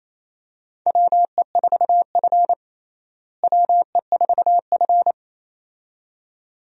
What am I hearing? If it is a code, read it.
Morse code, heard as a keyed tone, about 28 wpm, 700 Hz. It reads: WE4F WE4F
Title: USA Call Signs - 4 Characters - Repeated 2x - 28wpm